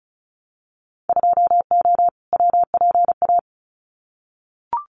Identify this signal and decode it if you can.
Morse code, heard as a keyed tone, about 34 wpm, 700 Hz. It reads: 2O WPA
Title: Winter Field Day Exchanges Encore 35wpm